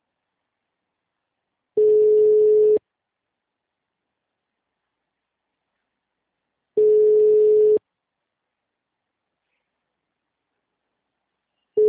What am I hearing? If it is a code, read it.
Ringback tone (Europe, 425 Hz).